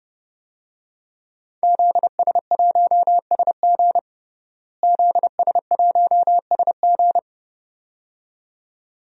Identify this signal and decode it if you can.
Morse code, heard as a keyed tone, about 30 wpm, 700 Hz. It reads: ZS1SG ZS1SG
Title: DXCC-100 Call Signs - 5 Characters - Repeated 2x - 30wpm